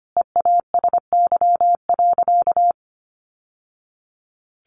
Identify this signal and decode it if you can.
Morse code, heard as a keyed tone, about 25 wpm, 700 Hz. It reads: EASY.
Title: Sentences from Top 700 Words 25wpm